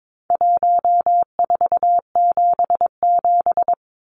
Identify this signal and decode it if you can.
Morse code, heard as a keyed tone, about 22 wpm, 700 Hz. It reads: 1477
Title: Numbers - Contest Style - 4-Digit 22wpm